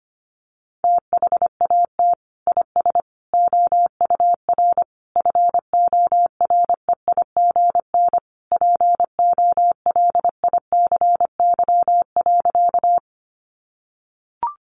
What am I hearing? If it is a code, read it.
Morse code, heard as a keyed tone, about 25 wpm, 700 Hz. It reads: THAT IS OUR FOREIGN POLICY.
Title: Sentences from Top 1400 Words 25wpm